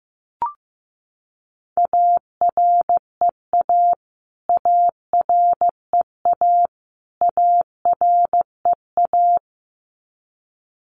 Morse code, heard as a keyed tone, about 15 wpm, 700 Hz. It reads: AREA AREA AREA
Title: Top 500 Words Repeated 15wpm